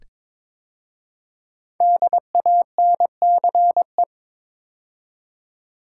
Morse code, heard as a keyed tone, about 22 wpm, 700 Hz. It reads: DANCE
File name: 5 Letter Words - First 3 Letters Given - 22wpm